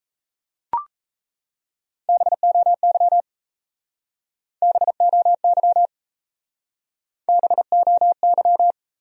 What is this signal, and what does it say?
Morse code, heard as a keyed tone, about 37 wpm, 700 Hz. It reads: BOY BOY BOY